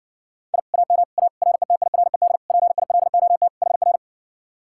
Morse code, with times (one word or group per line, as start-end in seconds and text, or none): 0.5-0.6 I
0.7-1.0 AM
1.2-1.3 A
1.4-2.4 RETIRED
2.5-3.5 PILOT
3.6-4.0 HR